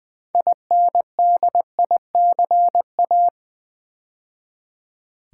Morse code, heard as a keyed tone, about 20 wpm, 700 Hz. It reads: INDICA
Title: Build A Word - 2 Letters At A Time - 8-Letter Words - 20wpm